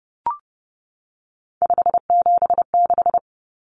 Morse code, heard as a keyed tone, about 30 wpm, 700 Hz. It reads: 576
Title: Common Cut Numbers - 3-Digit 30wpm